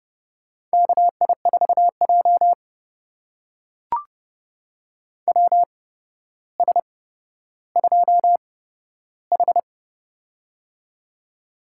Morse code, heard as a keyed tone, about 30 wpm, spaced as for 13 wpm, 700 Hz. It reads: KI4J WS2H
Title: USA Call Signs - 4 Characters - 8x Character Spacing - 30wpm